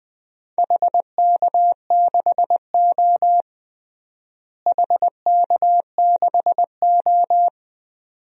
Morse code, heard as a keyed tone, about 20 wpm, 700 Hz. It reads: HK6O HK6O
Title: DXCC-100 Call Signs - 4 Characters - Repeated 2x - 20wpm